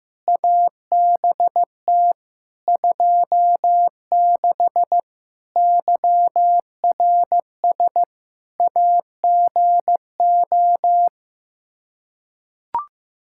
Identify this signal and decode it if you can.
Morse code, heard as a keyed tone, about 15 wpm, 700 Hz. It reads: ABT 26 YRS AGO
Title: K1USN SST QSO Practice 15wpm